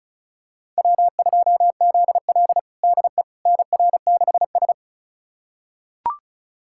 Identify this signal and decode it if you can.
Morse code, heard as a keyed tone, about 36 wpm, 700 Hz. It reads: W2ZL DE NR6S
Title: USA Callsign DE Callsign - 4 Characters - 35wpm